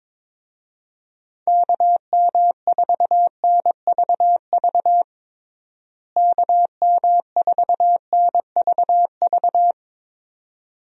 Morse code, heard as a keyed tone, about 21 wpm, 700 Hz. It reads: KM4NVV KM4NVV
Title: USA Call Signs - 6 Characters - Repeated 2x - 22wpm